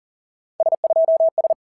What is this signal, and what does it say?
Morse code, heard as a keyed tone, about 40 wpm, 650 Hz. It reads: S2S
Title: SOTA QSO Practice 40wpm